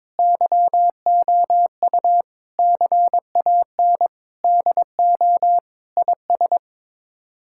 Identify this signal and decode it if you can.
Morse code, heard as a keyed tone, about 22 wpm, 700 Hz. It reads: YOU CAN DO IS
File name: practice / 22wpm / Sets of 7 Words - 22wpm